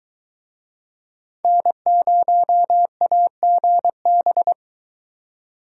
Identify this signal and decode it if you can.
Morse code, heard as a keyed tone, about 24 wpm, 700 Hz. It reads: N0AGB